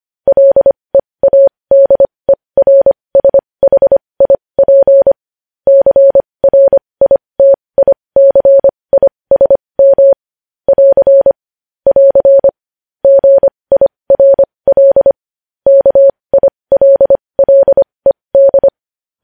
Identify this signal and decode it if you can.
Morse code, heard as a keyed tone, about 25 wpm, spaced as for 20 wpm, 550 Hz. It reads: LEADERSHIP CRITICISM <AR> <AR> GIRL KILLED
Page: News Headlines in Morse Code at 20 WPM